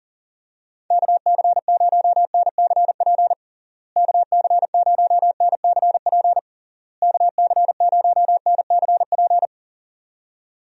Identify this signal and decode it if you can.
Morse code, heard as a keyed tone, about 37 wpm, 700 Hz. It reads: KC0NCP KC0NCP KC0NCP
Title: USA Call Signs - 6 Characters - Repeated 3x - 40wpm